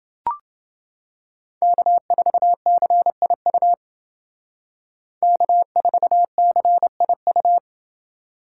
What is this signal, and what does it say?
Morse code, heard as a keyed tone, about 29 wpm, 700 Hz. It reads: K4CIU K4CIU